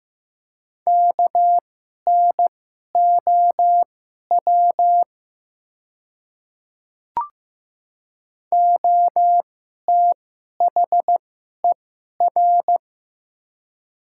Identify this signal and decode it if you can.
Morse code, heard as a keyed tone, about 15 wpm, spaced as for 11 wpm, 700 Hz. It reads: KNOW OTHER
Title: 2x Character Spacing - Top 300 Words - 15wpm